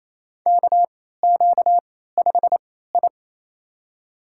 Morse code, heard as a keyed tone, about 28 wpm, spaced as for 16 wpm, 700 Hz. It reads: KQ5I
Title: USA Call Signs - 3x Character Spacing - 28wpm